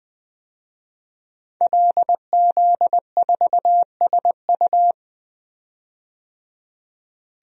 Morse code, heard as a keyed tone, about 20 wpm, 700 Hz. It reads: LZ4SU
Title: DXCC-100 Call Signs - 5 Characters - 20wpm